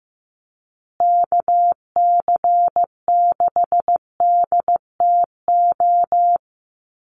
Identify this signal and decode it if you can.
Morse code, heard as a keyed tone, about 15 wpm, 700 Hz. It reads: KC6DTO
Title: USA Call Signs Repeated 3x 15wpm